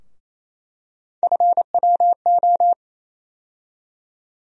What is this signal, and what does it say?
Morse code, heard as a keyed tone, about 28 wpm, 700 Hz. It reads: FWO
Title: USA Call Sign Postfixes - 3 Letters 28wpm